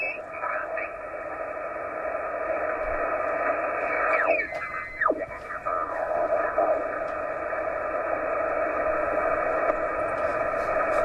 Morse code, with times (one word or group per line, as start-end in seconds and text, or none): none